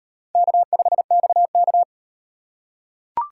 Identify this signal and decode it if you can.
Morse code, heard as a keyed tone, about 33 wpm, 700 Hz. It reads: K5XK